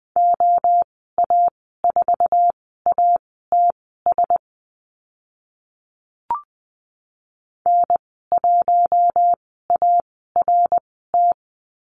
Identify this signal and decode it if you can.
Morse code, heard as a keyed tone, about 20 wpm, spaced as for 15 wpm, 700 Hz. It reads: OA4ATS N1ART
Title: 055 Lesson 10g - Letter r - Call Signs 20wpm